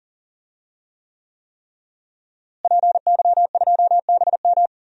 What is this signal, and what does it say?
Morse code, heard as a keyed tone, about 40 wpm, 700 Hz. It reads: PY2BM
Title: DXCC-100 Call Signs - 5 Characters - Repeated 3x - 40wpm